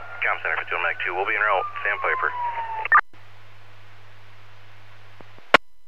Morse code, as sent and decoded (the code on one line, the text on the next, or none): none